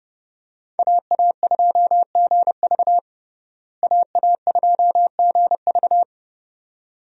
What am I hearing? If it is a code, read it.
Morse code, heard as a keyed tone, about 32 wpm, 700 Hz. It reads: AA2GV AA2GV
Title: USA Call Signs - 5 Characters - Repeated 2x - 30wpm